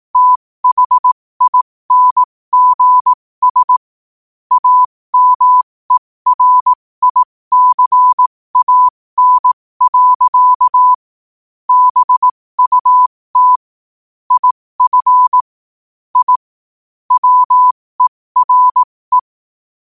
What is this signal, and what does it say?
Morse code, heard as a keyed tone, about 18 wpm, spaced as for 15 wpm, 1000 Hz. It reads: THINGS AMERICAN. BUT IF I WERE